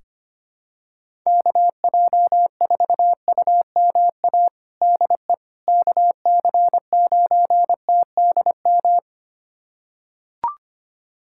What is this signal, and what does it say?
Morse code, heard as a keyed tone, about 25 wpm, 700 Hz. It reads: KJ4UMA DE KC9TDM
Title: USA Callsign DE Callsign - 6 Characters - 25wpm